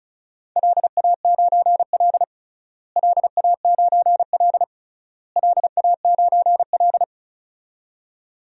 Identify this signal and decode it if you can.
Morse code, heard as a keyed tone, about 34 wpm, 700 Hz. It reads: LA9L LA9L LA9L
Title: DXCC-100 Call Signs - 4 Characters - Repeated 3x - 35wpm